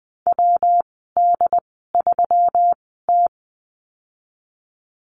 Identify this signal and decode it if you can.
Morse code, heard as a keyed tone, about 19 wpm, spaced as for 14 wpm, 700 Hz. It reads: WD3T
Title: 186 Lesson 30g - Letter b - Call Signs 20wpm